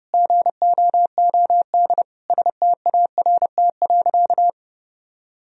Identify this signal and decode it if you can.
Morse code, heard as a keyed tone, about 30 wpm, 700 Hz. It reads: GOOD START.
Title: Sentences from Top 1000 Words - Review Part 1 - 30wpm